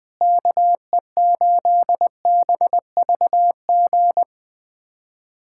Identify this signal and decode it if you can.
Morse code, heard as a keyed tone, about 20 wpm, 700 Hz. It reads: KE8BVG
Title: USA Callsign DE Callsign - 6 Characters - 20wpm